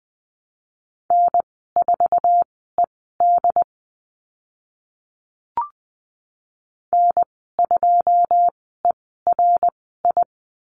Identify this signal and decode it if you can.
Morse code, heard as a keyed tone, about 20 wpm, spaced as for 14 wpm, 700 Hz. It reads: N4ED N2ERI
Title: 082 Lesson 14g - Number 2 - Call Signs 20wpm